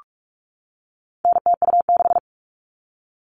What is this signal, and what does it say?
Morse code, heard as a keyed tone, about 44 wpm, 700 Hz. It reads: NTU6
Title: All Cut Numbers - 4-Digit 45wpm